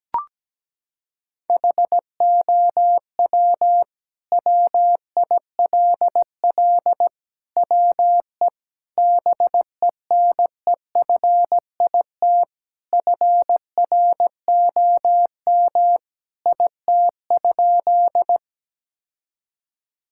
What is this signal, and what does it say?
Morse code, heard as a keyed tone, about 17 wpm, 700 Hz. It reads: HOW WILL WE BENEFIT FROM IT?
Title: Sentences from Top 1000 Words - Review Part 6 - 17wpm